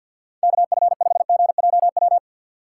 Morse code, heard as a keyed tone, about 63 wpm, 700 Hz. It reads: KF5GJW